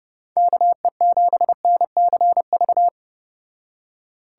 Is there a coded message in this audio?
Yes: Morse code from 0.4 to 2.9 s, heard as a keyed tone, about 30 wpm, 700 Hz. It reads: KE7NCV